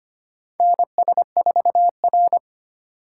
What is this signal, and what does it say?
Morse code, heard as a keyed tone, about 25 wpm, 700 Hz. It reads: NS4R
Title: USA Call Signs - 4 Characters - 3x Character Spacing - 25wpm